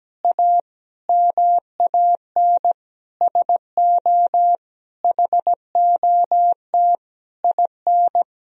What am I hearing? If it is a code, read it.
Morse code, heard as a keyed tone, about 17 wpm, 700 Hz. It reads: A MAN SO HOT IN
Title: Sentences from Top 1000 Words - Review Part 3 - 17wpm